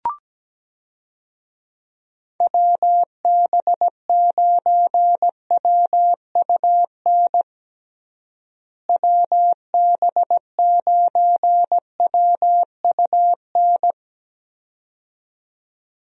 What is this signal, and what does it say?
Morse code, heard as a keyed tone, about 17 wpm, 700 Hz. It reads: WB9WUN WB9WUN
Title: USA Call Signs - 6 Characters - Repeated 2x - 17wpm